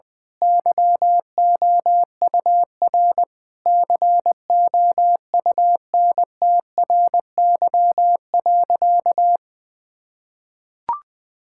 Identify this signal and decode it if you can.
Morse code, heard as a keyed tone, about 20 wpm, 700 Hz. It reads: YOUR COUNTRY.
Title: Sentences with 9-Word Phrases - 20wpm